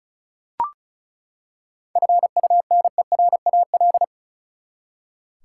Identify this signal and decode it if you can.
Morse code, heard as a keyed tone, about 36 wpm, 700 Hz. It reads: FUNERAL
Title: Seven Letter Words - 35wpm